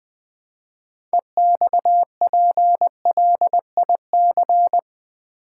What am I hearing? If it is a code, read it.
Morse code, heard as a keyed tone, about 20 wpm, 700 Hz. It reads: EXPLIC
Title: Build A Word - 3 Letters At A Time - 8-Letter Words - 20wpm